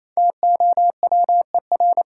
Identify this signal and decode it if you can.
Morse code, heard as a keyed tone, about 28 wpm, 700 Hz. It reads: TOWER
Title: Five Letter Words - 28wpm